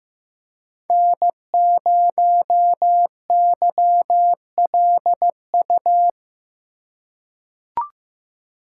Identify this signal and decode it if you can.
Morse code, heard as a keyed tone, about 15 wpm, 700 Hz. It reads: N0YLU
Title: USA Call Signs - 5 Characters - 2x Character Spacing - 15wpm